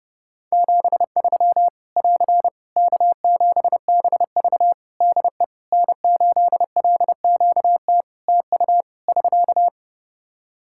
Morse code, heard as a keyed tone, about 30 wpm, 700 Hz. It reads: 73 <AR> K7BV DE N8LQT TU <SK>
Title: Practice QSO - Exchange 4 - Ending - 30wpm